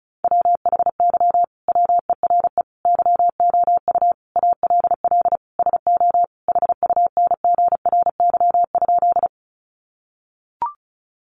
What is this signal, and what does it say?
Morse code, heard as a keyed tone, about 34 wpm, 700 Hz. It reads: WHY WERE YOU ALL SO HUNGRY?
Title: Sentences from Top 1400 Words 35wpm